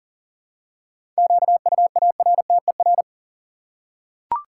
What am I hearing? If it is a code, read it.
Morse code, heard as a keyed tone, about 40 wpm, 700 Hz. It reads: QUARTER
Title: Seven Letter Words - 40wpm